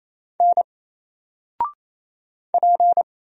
Morse code, heard as a keyed tone, about 28 wpm, 700 Hz. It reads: N P
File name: Single-Letter-Number-28wpm